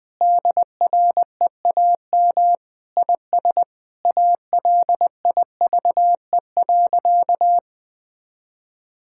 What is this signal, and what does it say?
Morse code, heard as a keyed tone, about 20 wpm, 700 Hz. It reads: DREAM IS ALIVE.
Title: Sentences with 2-Word Phrases - 20wpm